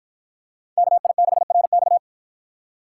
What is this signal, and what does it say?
Morse code, heard as a keyed tone, about 53 wpm, 700 Hz. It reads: KI6RX